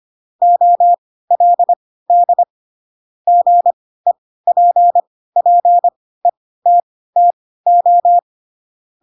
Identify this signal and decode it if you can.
Morse code, heard as a keyed tone, about 25 wpm, spaced as for 16 wpm, 700 Hz. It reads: OLD GEPPETTO
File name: Pinocchio / Pinocchio ch4 10 25 ik1hge